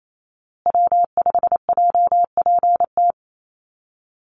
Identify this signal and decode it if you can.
Morse code, heard as a keyed tone, about 28 wpm, 700 Hz. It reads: W5JPT
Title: USA Call Signs Repeated 2x 28wpm